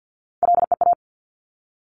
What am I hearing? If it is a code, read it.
Morse code, heard as a keyed tone, about 50 wpm, 690 Hz. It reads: LEA
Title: Top Three Letter Combinations in Words 50wpm